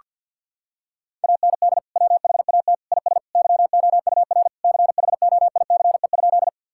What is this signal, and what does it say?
Morse code, heard as a keyed tone, about 48 wpm, 700 Hz. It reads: AND WHAT IS YOUR CHOICE?
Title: Sentences from Top 1000 Words - Review Part 6 - 50wpm